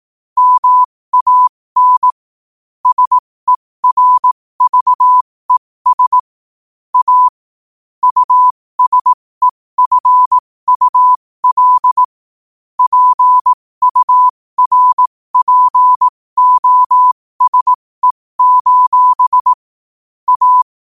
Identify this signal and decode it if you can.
Morse code, heard as a keyed tone, about 18 wpm, spaced as for 15 wpm, 1000 Hz. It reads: MAN SERVES A USEFUL PURPOSE: A
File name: qotdfiles / 15wpm